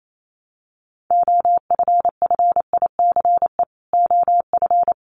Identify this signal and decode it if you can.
Morse code, heard as a keyed tone, about 28 wpm, 700 Hz. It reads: OFFICE OF
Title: Sets of 5 Words Encore2 28wpm